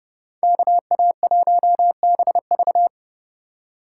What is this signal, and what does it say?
Morse code, heard as a keyed tone, about 30 wpm, 700 Hz. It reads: KA1BV
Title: USA Call Signs - 5 Characters - 4x Character Spacing - 30wpm